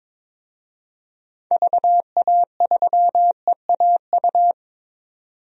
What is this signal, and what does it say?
Morse code, heard as a keyed tone, about 22 wpm, 700 Hz. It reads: VA3EAU
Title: DXCC-100 Call Signs - 6 Characters - 22wpm